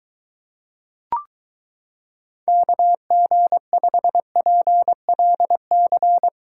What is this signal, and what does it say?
Morse code, heard as a keyed tone, about 23 wpm, 700 Hz. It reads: KG5PLC